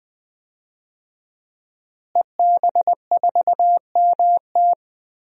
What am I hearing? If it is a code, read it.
Morse code, heard as a keyed tone, about 20 wpm, 700 Hz. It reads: EB4MT
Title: DXCC-100 Call Signs - 5 Characters - Repeated 3x - 20wpm